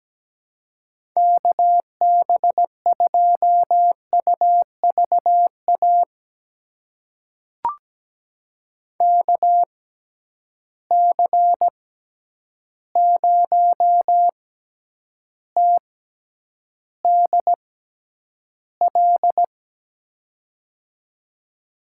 Morse code, heard as a keyed tone, about 17 wpm, spaced as for 9 wpm, 700 Hz. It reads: KB2UVA KC0TDL
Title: USA Call Signs - 6 Characters - 6x Character Spacing - 17wpm